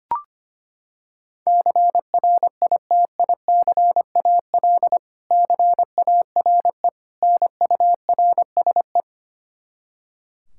Morse code, heard as a keyed tone, about 25 wpm, 700 Hz. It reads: CRITICAL CARE NURSE
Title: Practice QSO - Exchange 3 - More Info - 25wpm